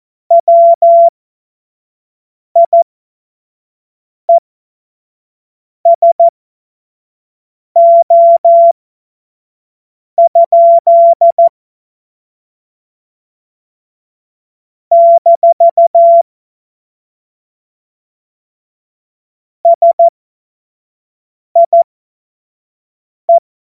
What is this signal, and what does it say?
Morse code, heard as a keyed tone, about 14 wpm, spaced as for 5 wpm, 680 Hz. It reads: WIESO? - SIE
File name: Morsekurs